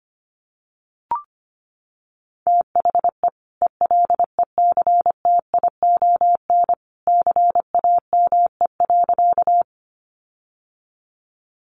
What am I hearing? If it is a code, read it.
Morse code, heard as a keyed tone, about 25 wpm, 700 Hz. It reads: THE ELECTION CAME.
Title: Sentences from Top 1500 Words 25wpm